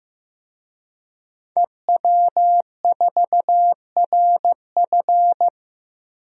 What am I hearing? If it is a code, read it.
Morse code, heard as a keyed tone, about 15 wpm, 700 Hz. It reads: EW4RF
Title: DXCC-100 Call Signs - 5 Characters - 15wpm